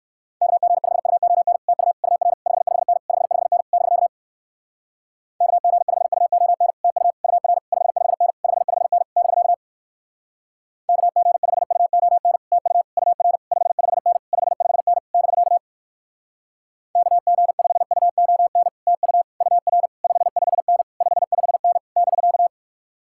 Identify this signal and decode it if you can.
Morse code, heard as a keyed tone, about 67 wpm, 700 Hz. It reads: KG5UON TU UR 55N 55N <BK> KG5UON TU UR 55N 55N <BK> KG5UON TU UR 55N 55N <BK> KG5UON TU UR 55N 55N <BK>